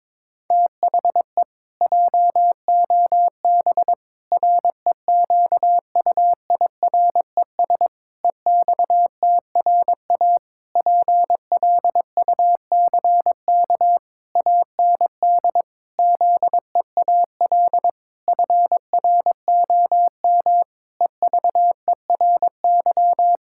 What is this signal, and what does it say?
Morse code, heard as a keyed tone, about 22 wpm, 700 Hz. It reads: THE JOB REQUIRES EXTRA PLUCK AND ZEAL FROM EVERY
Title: Pangram Challenge v2 - 22wpm